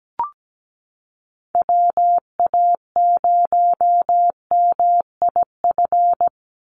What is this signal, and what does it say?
Morse code, heard as a keyed tone, about 17 wpm, 700 Hz. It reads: WA0MIF
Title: USA Call Signs 17wpm